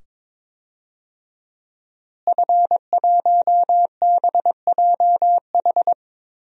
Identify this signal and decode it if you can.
Morse code, heard as a keyed tone, about 22 wpm, 700 Hz. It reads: F1BJH
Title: DXCC-100 Call Signs - 5 Characters - Repeated 2x - 22wpm